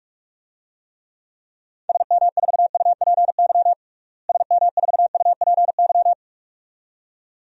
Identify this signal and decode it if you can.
Morse code, heard as a keyed tone, about 41 wpm, 700 Hz. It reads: SM4UPY SM4UPY
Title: DXCC-100 Call Signs - 6 Characters - Repeated 2x - 45wpm